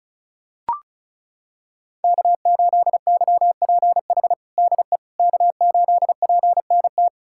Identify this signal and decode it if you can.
Morse code, heard as a keyed tone, about 36 wpm, 690 Hz. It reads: K8YPH DE K8PNT
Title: USA Callsign DE Callsign - 5 Characters - 35wpm